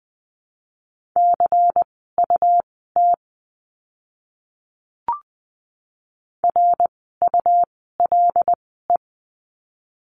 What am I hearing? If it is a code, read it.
Morse code, heard as a keyed tone, about 20 wpm, spaced as for 14 wpm, 700 Hz. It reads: CUT RULE
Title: 101 Lesson 17f - Letter u - Words 20wpm